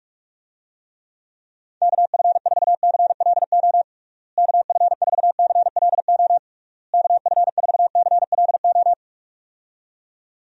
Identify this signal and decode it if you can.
Morse code, heard as a keyed tone, about 46 wpm, 700 Hz. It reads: KF4CLO KF4CLO KF4CLO
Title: USA Call Signs - 6 Characters - Repeated 3x - 45wpm